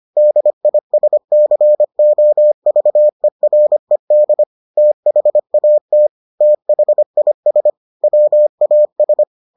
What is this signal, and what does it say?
Morse code, heard as a keyed tone, about 25 wpm, 600 Hz. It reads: DISCOVERED THAT THIS WAS